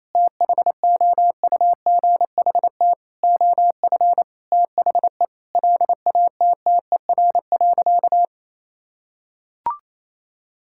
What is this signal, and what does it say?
Morse code, heard as a keyed tone, about 28 wpm, 700 Hz. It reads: THOUGHT OF THE LATTER.
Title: Sentences from Top 1600 Words 28wpm